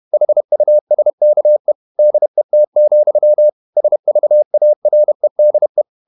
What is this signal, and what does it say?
Morse code, heard as a keyed tone, about 31 wpm, 600 Hz. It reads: HUSKE DET, SVAREDE
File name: Ferie QRQ 31wpm